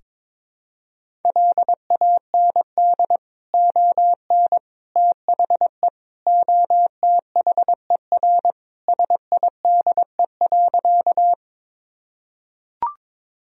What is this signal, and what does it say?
Morse code, heard as a keyed tone, about 22 wpm, 700 Hz. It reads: LAND ON THE OTHER SIDE.
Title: Sentences from Top 1000 Words - Review Part 1 - 22wpm